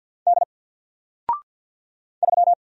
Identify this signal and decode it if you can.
Morse code, heard as a keyed tone, about 50 wpm, 700 Hz. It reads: D 3
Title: Single Letter-Number 50wpm